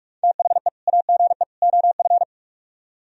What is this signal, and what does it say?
Morse code, heard as a keyed tone, about 46 wpm, 700 Hz. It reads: THE AGE OF